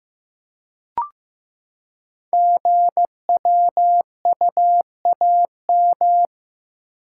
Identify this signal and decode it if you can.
Morse code, heard as a keyed tone, about 15 wpm, 700 Hz. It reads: GWUAM
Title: Code Groups - 5 Letters - 15wpm